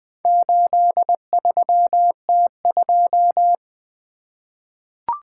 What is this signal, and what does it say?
Morse code, heard as a keyed tone, about 20 wpm, 700 Hz. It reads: 83T2
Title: Four Letter-Number Combinations - 20wpm